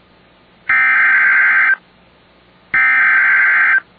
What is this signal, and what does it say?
Emergency Alert System (SAME) tones (header bursts), heard as harsh buzzing bursts. It is heard from 0.7 to 3.7 s.